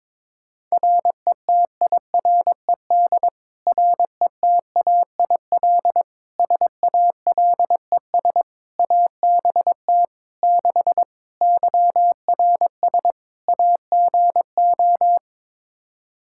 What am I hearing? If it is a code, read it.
Morse code, heard as a keyed tone, about 22 wpm, 700 Hz. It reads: RETIRED RETAIL SALES ABT 6 YRS AGO
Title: Practice QSO - Exchange 3 - More Info - 22wpm